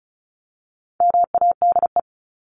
Morse code, heard as a keyed tone, about 34 wpm, 700 Hz. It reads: MADE
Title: Top 300 Words Repeated 35wpm